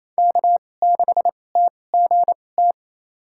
Morse code, heard as a keyed tone, about 28 wpm, spaced as for 20 wpm, 700 Hz. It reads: K6TGT
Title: USA Call Signs - 5 Characters - 2x Character Spacing - 28wpm